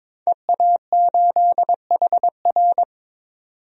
Morse code, heard as a keyed tone, about 21 wpm, 700 Hz. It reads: EA8HR
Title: DXCC-100 Call Signs - 5 Characters - 22wpm